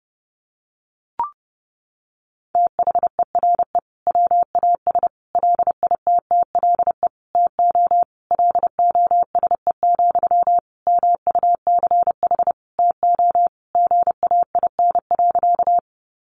Morse code, heard as a keyed tone, about 30 wpm, 700 Hz. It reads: THERE WAS LITTLE TO LOSE, MUCH TO GAIN.
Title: Sentences from Top 1100 Words 30wpm